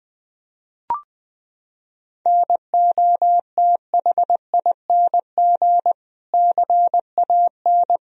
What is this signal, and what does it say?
Morse code, heard as a keyed tone, about 21 wpm, 700 Hz. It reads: NOTHING CAN
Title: Sentences from Top 1000 Words - Review Part 2 - 20wpm